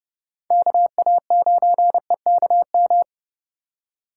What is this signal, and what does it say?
Morse code, heard as a keyed tone, about 31 wpm, 700 Hz. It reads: KA9EKM
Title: USA Call Signs - 2x Character Spacing - 30wpm